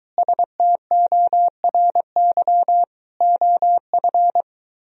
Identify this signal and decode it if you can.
Morse code, heard as a keyed tone, about 23 wpm, 700 Hz. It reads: STORY OF